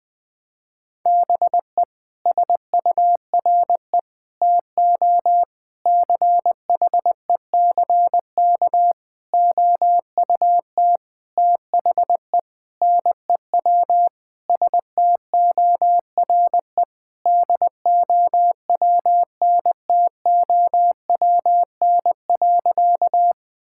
Morse code, heard as a keyed tone, about 20 wpm, 700 Hz. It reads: BE SURE TO CHECK OUT THE NEW STORE DOWNTOWN.
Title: Sentences with 6-Word Phrases - Spoken Beginning - 20wpm